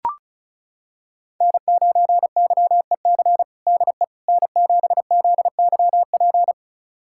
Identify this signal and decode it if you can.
Morse code, heard as a keyed tone, about 34 wpm, 690 Hz. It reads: N9YEC DE N7ZYP
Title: USA Callsign DE Callsign - 5 Characters - 35wpm